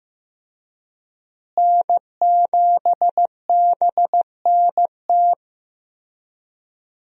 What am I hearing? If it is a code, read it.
Morse code, heard as a keyed tone, about 15 wpm, 700 Hz. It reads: N7BNT
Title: USA Call Signs - 5 Characters - Repeated 2x - 15wpm